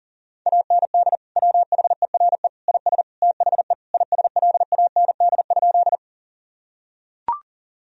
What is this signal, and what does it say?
Morse code, heard as a keyed tone, about 40 wpm, 700 Hz. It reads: AND WHERE IS THE ISLAND?
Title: Sentences from Top 1000 Words - Review Part 3 - 40wpm